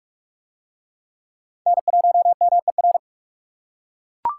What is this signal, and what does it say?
Morse code, heard as a keyed tone, about 39 wpm, 700 Hz. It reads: N1MER